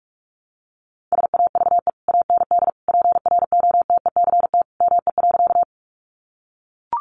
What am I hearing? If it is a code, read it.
Morse code, heard as a keyed tone, about 44 wpm, 700 Hz. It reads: SAVE AND PROTECT ME.
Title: Sentences from Top 1100 Words 45wpm